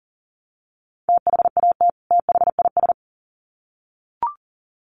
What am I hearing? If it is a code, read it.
Morse code, heard as a keyed tone, about 39 wpm, 700 Hz. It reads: THAT THIS
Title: Sets of 2 Words 40wpm